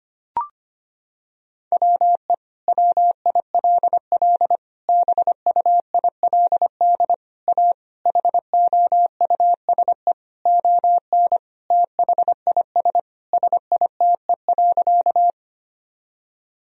Morse code, heard as a keyed tone, about 25 wpm, 700 Hz. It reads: WE WILL BUILD A HOUSE ON THIS SITE.
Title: Sentences from Top 1000 Words - Review Part 4 - 25wpm